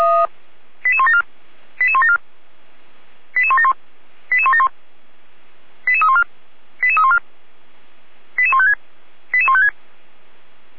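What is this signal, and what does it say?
Five-tone selective call (ZVEI), heard as quick runs of five tones, tones of about 70 ms. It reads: 88174 88174 88171 88171 88314 88314 88156 88156
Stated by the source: ZVEI five-tone selective call received by SDR.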